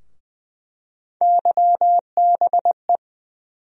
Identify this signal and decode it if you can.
Morse code, heard as a keyed tone, about 20 wpm, 700 Hz. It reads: YBE
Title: USA Call Sign Postfixes - 3 Letters 20wpm